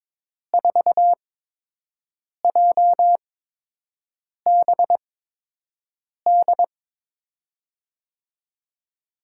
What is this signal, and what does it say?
Morse code, heard as a keyed tone, about 22 wpm, spaced as for 6 wpm, 700 Hz. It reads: 4JBD
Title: USA Call Signs - 5 Characters - 8x Character Spacing - 22wpm